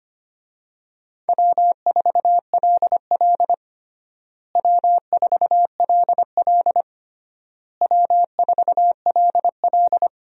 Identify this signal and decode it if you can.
Morse code, heard as a keyed tone, about 25 wpm, 700 Hz. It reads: W4LL W4LL W4LL
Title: USA Call Signs - 4 Characters - Repeated 3x - 25wpm